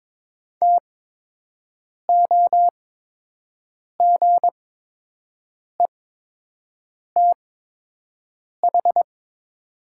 Morse code, heard as a keyed tone, about 22 wpm, spaced as for 6 wpm, 700 Hz. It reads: TOGETH
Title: 8x Character Spacing - Top 300 Words - 22wpm